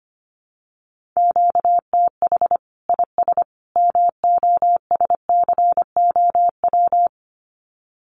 Morse code, heard as a keyed tone, about 25 wpm, 700 Hz. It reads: QTH IS MOSCOW
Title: Common QSO Phrases 25wpm